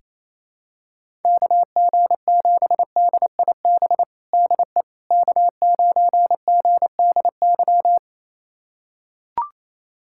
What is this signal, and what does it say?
Morse code, heard as a keyed tone, about 28 wpm, 700 Hz. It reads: KG7DIB DE K9GDY
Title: USA Callsign DE Callsign 28wpm